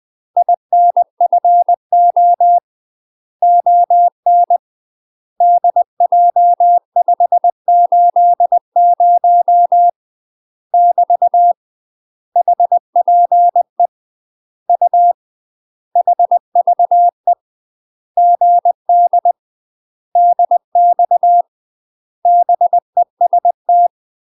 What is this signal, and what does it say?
Morse code, heard as a keyed tone, about 20 wpm, 700 Hz. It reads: INFO ON DJ580 <BT> HPE U HVE GD DX BEST
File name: UK MASTER QSOs TRAINING / 20WPM / UK MASTER QSOs 0049